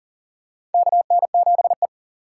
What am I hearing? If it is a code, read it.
Morse code, heard as a keyed tone, about 40 wpm, 700 Hz. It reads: KN7E
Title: USA Call Signs - 4 Characters - 3x Character Spacing - 40wpm